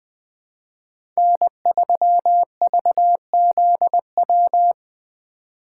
Morse code, heard as a keyed tone, about 20 wpm, 700 Hz. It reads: N3VZW
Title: USA Call Signs - 5 Characters - 4x Character Spacing - 20wpm